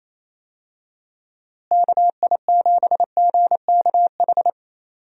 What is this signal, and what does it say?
Morse code, heard as a keyed tone, about 28 wpm, 700 Hz. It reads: KI7GKH
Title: USA Call Signs - 6 Characters - Repeated 2x - 28wpm